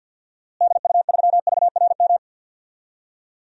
Morse code, heard as a keyed tone, about 51 wpm, 700 Hz. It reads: DU3VRM
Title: DXCC-100 Call Signs - 6 Characters - Repeated 3x - 50wpm